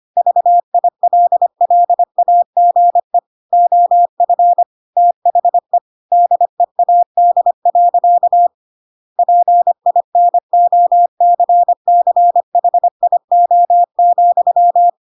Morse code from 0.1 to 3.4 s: VILLAGE